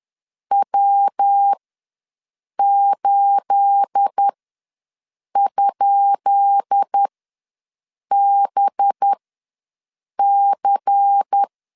Morse code, heard as a keyed tone, about 10 wpm, spaced as for 6 wpm, 800 Hz. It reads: W8?BC
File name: LEZIONI / 10 SIGNS / 10B INT ws 10wpm 4134